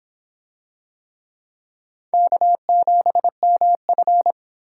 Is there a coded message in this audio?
Yes: Morse code from 2.1 to 4.3 s, heard as a keyed tone, about 25 wpm, 700 Hz. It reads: K7MF